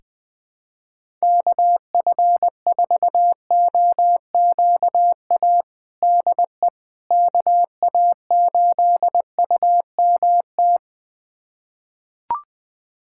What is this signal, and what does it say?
Morse code, heard as a keyed tone, about 20 wpm, 700 Hz. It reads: KF4OQA DE KA8UMT
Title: USA Callsign DE Callsign - 6 Characters - 20wpm